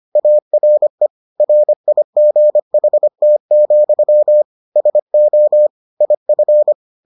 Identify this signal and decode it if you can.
Morse code, heard as a keyed tone, about 25 wpm, 600 Hz. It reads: ARE RIGHT, SO IF